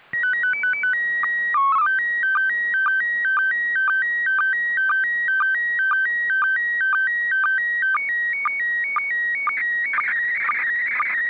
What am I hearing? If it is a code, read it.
SSTV (slow-scan television), PD 120.
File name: ik1yrc / AUDIO FOR QRZ / FLUSSO ISS 29 01 2021 05 00 00 UTC